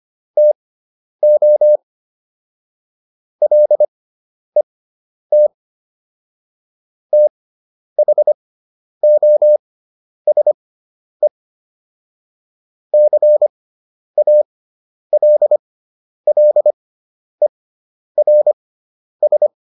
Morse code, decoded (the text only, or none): TO LET THOSE CALLERS